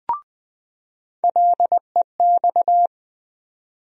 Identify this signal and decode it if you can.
Morse code, heard as a keyed tone, about 20 wpm, 700 Hz. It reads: LEX
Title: USA Call Sign Postfixes - 3 Letters 20wpm